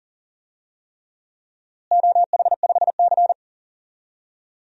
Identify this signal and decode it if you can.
Morse code, heard as a keyed tone, about 38 wpm, 700 Hz. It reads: OH5C
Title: DXCC-100 Call Signs - 4 Characters - Repeated 3x - 40wpm